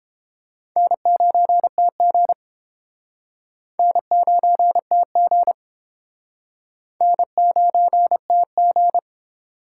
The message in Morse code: N9TG N9TG N9TG

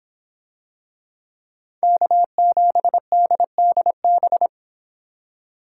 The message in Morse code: K7DDB

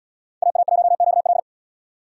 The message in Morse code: AA1QL

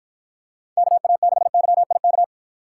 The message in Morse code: KA6YIK

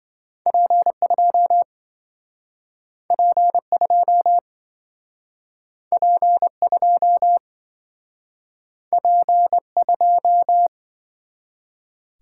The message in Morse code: P2 P2 P2 P2

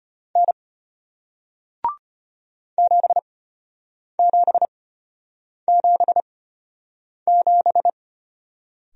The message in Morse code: N 7777

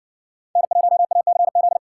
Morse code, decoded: N1ACZ

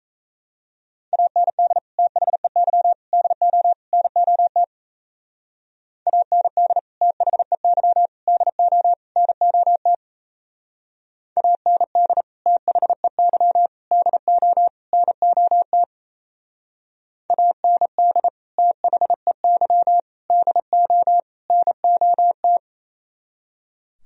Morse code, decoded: AND THEY DO NOT AND THEY DO NOT AND THEY DO NOT AND THEY DO NOT

